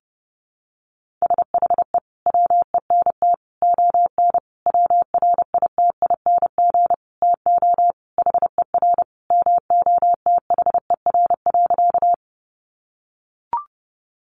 Morse code, heard as a keyed tone, about 30 wpm, 700 Hz. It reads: SHE WENT ON WRITING TO HER MOTHER.
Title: Sentences from Top 800 Words 30wpm